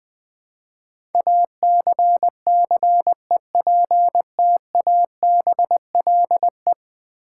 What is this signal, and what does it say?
Morse code, heard as a keyed tone, about 20 wpm, 700 Hz. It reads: ACCEPTABLE
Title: Build A Word - 4 Letters At A Time - 10-Letter Words - 20wpm